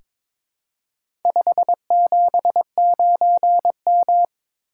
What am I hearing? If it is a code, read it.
Morse code, heard as a keyed tone, about 22 wpm, 700 Hz. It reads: 579M
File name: practice / 22wpm / Four Letter-Number Combinations - 22wpm